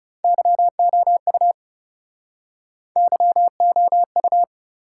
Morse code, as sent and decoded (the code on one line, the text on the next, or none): -.-- --- ..- / -.-- --- ..-
YOU YOU